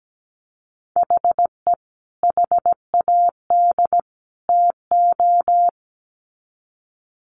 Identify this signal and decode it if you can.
Morse code, heard as a keyed tone, about 17 wpm, 700 Hz. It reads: HE HAD TO
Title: Sets of 3 Words 17wpm